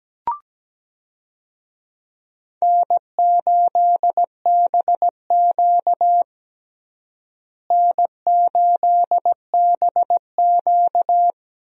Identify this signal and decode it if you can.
Morse code, heard as a keyed tone, about 17 wpm, 700 Hz. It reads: N8BQ N8BQ
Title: USA Call Signs - 4 Characters - Repeated 3x - 17wpm